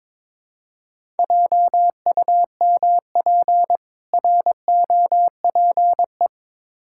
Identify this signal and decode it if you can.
Morse code, heard as a keyed tone, about 22 wpm, 700 Hz. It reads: JUMP ROPE
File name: Difficult Words with Hints - 22wpm